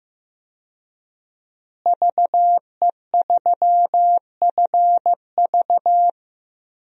Morse code, heard as a keyed tone, about 15 wpm, 700 Hz. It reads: VE3FV
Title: DXCC-100 Call Signs - 5 Characters - Repeated 3x - 15wpm